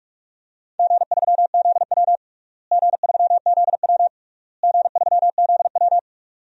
Morse code, heard as a keyed tone, about 46 wpm, 700 Hz. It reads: G3ZW G3ZW G3ZW
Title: DXCC-100 Call Signs - 4 Characters - Repeated 3x - 45wpm